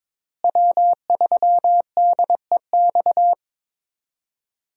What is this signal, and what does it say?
Morse code, heard as a keyed tone, about 22 wpm, 700 Hz. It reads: W3DEX
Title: USA Call Signs - 5 Characters - Repeated 2x - 22wpm